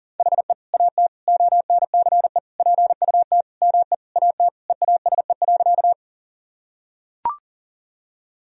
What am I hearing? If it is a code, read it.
Morse code, heard as a keyed tone, about 40 wpm, 700 Hz. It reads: HE AT ONCE PUT ME AT EASE.
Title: Sentences from Top 1000 Words - Review Part 1 - 40wpm